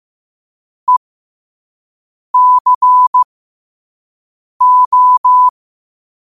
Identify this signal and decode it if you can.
Morse code, heard as a keyed tone, about 15 wpm, spaced as for 5 wpm, 1000 Hz. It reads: ECO
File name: qotdfiles / 5wpm